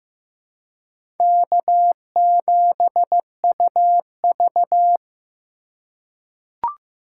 Morse code, heard as a keyed tone, about 15 wpm, 700 Hz. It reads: K7UV
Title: USA Call Signs - 4 Characters - 4x Character Spacing - 15wpm